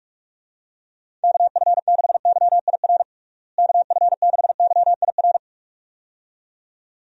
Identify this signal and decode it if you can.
Morse code, heard as a keyed tone, about 41 wpm, 700 Hz. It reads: KF6YIR KF6YIR
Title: USA Call Signs - 6 Characters - Repeated 2x - 45wpm